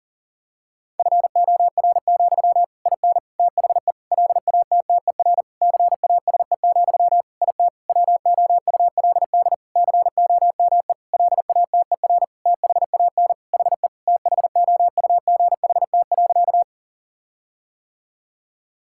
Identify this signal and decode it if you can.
Morse code, heard as a keyed tone, about 40 wpm, 700 Hz. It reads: FOR, IN THE LATTER CASE, IT WOULD COME LATER THAN HE THOUGHT.
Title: Sentences from Top 1600 Words 40wpm